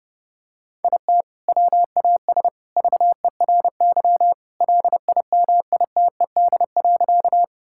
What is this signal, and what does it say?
Morse code, heard as a keyed tone, about 32 wpm, 700 Hz. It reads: IT WAS VERY LIMITED.
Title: Sentences from Top 1000 Words - Review Part 6 - 30wpm